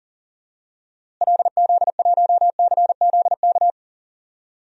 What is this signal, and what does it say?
Morse code, heard as a keyed tone, about 40 wpm, 700 Hz. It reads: LZ1CZK
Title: DXCC-100 Call Signs - 6 Characters - Repeated 3x - 40wpm